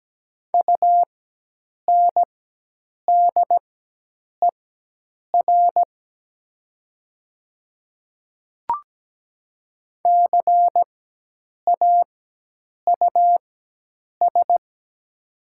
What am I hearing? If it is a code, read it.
Morse code, heard as a keyed tone, about 17 wpm, spaced as for 8 wpm, 700 Hz. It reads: UNDER CAUS
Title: 4x Character Spacing - Top 200 Words - 17wpm